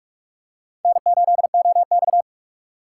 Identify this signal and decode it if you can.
Morse code, heard as a keyed tone, about 41 wpm, 700 Hz. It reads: N8OX
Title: USA Call Signs - 6x Character Spacing - 45wpm